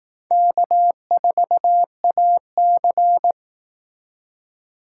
Morse code, heard as a keyed tone, about 18 wpm, 700 Hz. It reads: K4AC
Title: Speed Racing - USA 1x2 Call Signs 15wpm